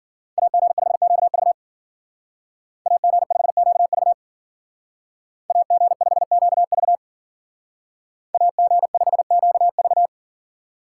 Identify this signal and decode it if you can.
Morse code, heard as a keyed tone, about 51 wpm, 700 Hz. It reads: AG5QV AG5QV AG5QV AG5QV